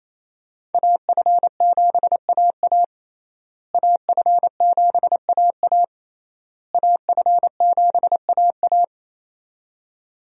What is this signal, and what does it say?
Morse code, heard as a keyed tone, about 28 wpm, 700 Hz. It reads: AF7AA AF7AA AF7AA
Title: USA Call Signs - 5 Characters - Repeated 3x - 28wpm